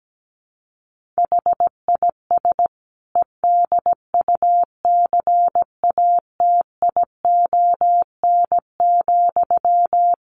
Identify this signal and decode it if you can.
Morse code, heard as a keyed tone, about 17 wpm, 700 Hz. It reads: HIS EDUCATION,
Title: Sentences from Top 1400 Words 17wpm